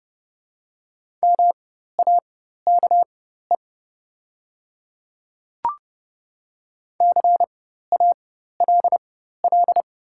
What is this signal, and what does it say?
Morse code, heard as a keyed tone, about 30 wpm, spaced as for 14 wpm, 700 Hz. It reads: MAKE CALL
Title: 4x Character Spacing - Top 100 Words - 30wpm